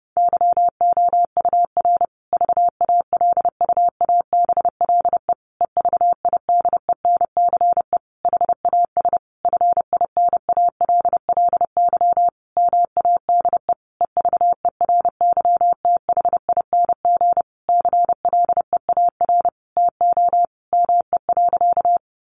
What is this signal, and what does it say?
Morse code, heard as a keyed tone, about 30 wpm, 700 Hz. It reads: YOUR VALUABLE EVIDENCE HAS FINALLY MADE EVERYTHING CLEAR TO ME.
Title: Sentences from Top 1300 Words 30wpm